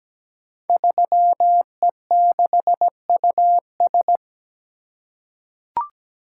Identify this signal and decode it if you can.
Morse code, heard as a keyed tone, about 17 wpm, 700 Hz. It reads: 3E6US
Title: Code Groups - 5 Letter:Numbers - 17wpm